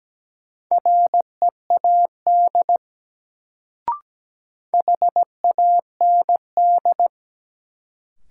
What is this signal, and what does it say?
Morse code, heard as a keyed tone, about 17 wpm, 700 Hz. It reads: READ HAND
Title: Sentences from Top 200 Words - New Word Drill - 17wpm